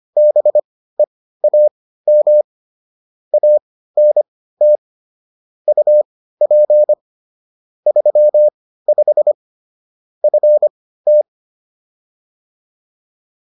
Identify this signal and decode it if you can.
Morse code, heard as a keyed tone, about 25 wpm, spaced as for 15 wpm, 600 Hz. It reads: BEAM ANT UP 35 FT